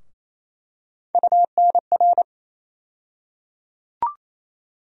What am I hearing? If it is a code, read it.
Morse code, heard as a keyed tone, about 27 wpm, 700 Hz. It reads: UNR
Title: USA Call Sign Postfixes - 3 Letters 28wpm